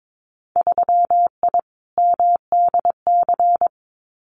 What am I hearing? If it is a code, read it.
Morse code, heard as a keyed tone, about 22 wpm, 700 Hz. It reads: 3I MDC
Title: Winter Field Day Exchanges Encore 22wpm